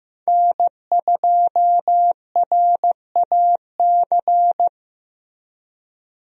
Morse code, heard as a keyed tone, about 15 wpm, 700 Hz. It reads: N2RAC
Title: USA Call Signs - 5 Characters - Repeated 3x - 15wpm